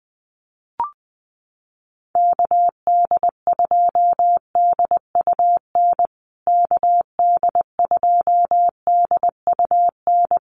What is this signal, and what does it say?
Morse code, heard as a keyed tone, about 20 wpm, 700 Hz. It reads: KD2DUN KD2DUN
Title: USA Call Signs Repeated 2x 20wpm